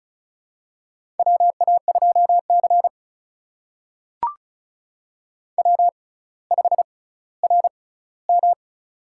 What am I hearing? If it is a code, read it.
Morse code, heard as a keyed tone, about 36 wpm, spaced as for 18 wpm, 700 Hz. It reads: WA2C W5RM
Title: USA Call Signs - 4 Characters - 6x Character Spacing - 35wpm